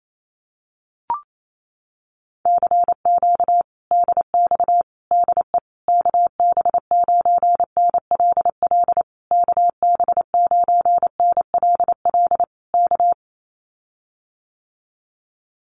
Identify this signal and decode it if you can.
Morse code, heard as a keyed tone, about 28 wpm, 700 Hz. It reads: CQ DX DE KB9NLL KB9NLL K
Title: Common QSO Phrases Encore 28wpm